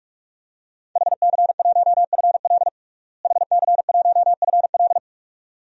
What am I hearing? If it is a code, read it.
Morse code, heard as a keyed tone, about 42 wpm, 700 Hz. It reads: HC1FL HC1FL
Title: DXCC-100 Call Signs - 5 Characters - Repeated 2x - 45wpm